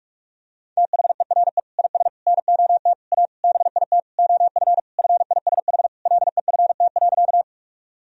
Morse code, heard as a keyed tone, about 46 wpm, 700 Hz. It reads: THERE IS NOT A BIT OF FISH LEFT.
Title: Sentences from Top 1000 Words - Review Part 1 - 45wpm